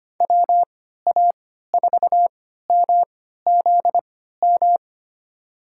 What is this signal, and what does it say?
Morse code, heard as a keyed tone, about 25 wpm, spaced as for 14 wpm, 700 Hz. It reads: WA4MZM
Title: USA Call Signs - 6 Characters - 3x Character Spacing - 25wpm